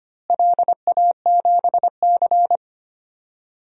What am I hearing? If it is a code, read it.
Morse code, heard as a keyed tone, about 25 wpm, 700 Hz. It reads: LA7C
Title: DXCC-100 Call Signs - 4 Characters - Repeated 3x - 25wpm